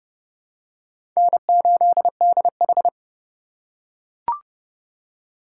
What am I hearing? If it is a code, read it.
Morse code, heard as a keyed tone, about 30 wpm, 700 Hz. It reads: N8DH
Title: USA Call Signs - 4 Characters - 2x Character Spacing - 30wpm